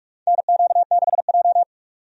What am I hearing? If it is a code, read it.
Morse code, heard as a keyed tone, about 46 wpm, 700 Hz. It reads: NQ6J
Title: USA Call Signs - 4 Characters - 45wpm